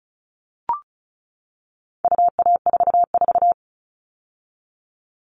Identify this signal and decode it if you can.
Morse code, heard as a keyed tone, about 34 wpm, 700 Hz. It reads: UA44
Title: All Cut Numbers - 4-Digit 35wpm